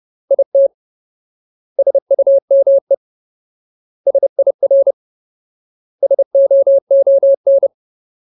Morse code, heard as a keyed tone, about 30 wpm, 550 Hz. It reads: IT SUME SIR SOON
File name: Ultimate Word List / 30WPM Ultimate